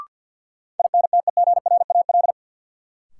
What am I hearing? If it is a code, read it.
Morse code, heard as a keyed tone, about 53 wpm, 700 Hz. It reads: INTEGRAL
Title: Sets of 5 Words - New Word Drill - 50wpm